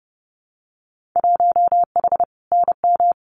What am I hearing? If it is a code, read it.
Morse code, heard as a keyed tone, about 30 wpm, 700 Hz. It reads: 1H NM
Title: Winter Field Day Exchanges 30wpm